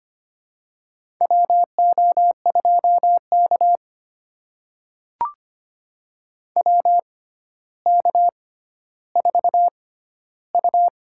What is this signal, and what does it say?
Morse code, heard as a keyed tone, about 25 wpm, spaced as for 13 wpm, 700 Hz. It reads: WO2K WK4U
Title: USA Call Signs - 6x Character Spacing - 25wpm